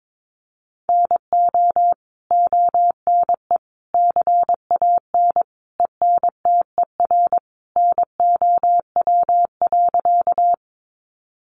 Morse code, heard as a keyed tone, about 22 wpm, 700 Hz. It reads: NO ONE CAN ENTER NOW.
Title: Sentences from Top 1300 Words 22wpm